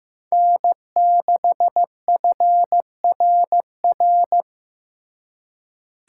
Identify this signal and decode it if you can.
Morse code, heard as a keyed tone, about 18 wpm, 700 Hz. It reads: N6FRR
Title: Speed Racing - USA 1x3 Call Signs 15wpm